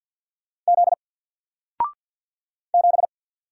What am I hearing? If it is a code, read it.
Morse code, heard as a keyed tone, about 50 wpm, 700 Hz. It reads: Z 7
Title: Single Letter-Number 50wpm